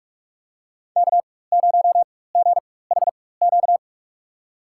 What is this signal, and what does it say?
Morse code, heard as a keyed tone, about 44 wpm, spaced as for 21 wpm, 700 Hz. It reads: K0GHQ
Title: USA Call Signs - 5 Characters - 4x Character Spacing - 45wpm